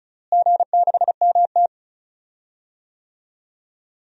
Morse code, heard as a keyed tone, about 36 wpm, 700 Hz. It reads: G6MT
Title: DXCC-100 Call Signs - 4 Characters - Repeated 2x - 35wpm